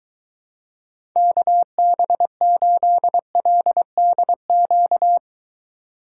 Morse code, heard as a keyed tone, about 22 wpm, 700 Hz. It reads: KB8LDQ